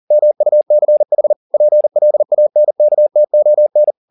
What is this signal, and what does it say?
Morse code, heard as a keyed tone, about 40 wpm, 600 Hz. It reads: MUCH PLANKTON